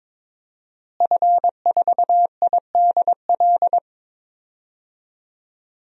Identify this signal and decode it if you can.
Morse code, heard as a keyed tone, about 21 wpm, 700 Hz. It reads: F4IDL
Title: DXCC-100 Call Signs - 5 Characters - Repeated 2x - 22wpm